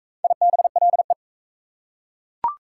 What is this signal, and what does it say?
Morse code, heard as a keyed tone, about 41 wpm, 700 Hz. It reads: IBLE